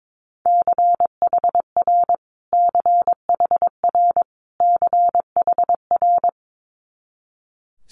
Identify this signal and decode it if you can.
Morse code, heard as a keyed tone, about 22 wpm, 700 Hz. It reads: CHR CHR CHR
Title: Multi-letter Phonemes Repeated 3x 22wpm